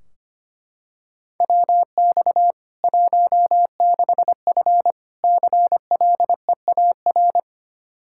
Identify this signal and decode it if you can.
Morse code, heard as a keyed tone, about 25 wpm, 700 Hz. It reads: WX 16F CLEAR
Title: K1USN SST QSO Practice 25wpm